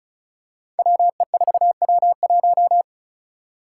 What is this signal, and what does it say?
Morse code, heard as a keyed tone, about 36 wpm, 700 Hz. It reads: WE4W1
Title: Five Challenging Letter-Number Combinations 35wpm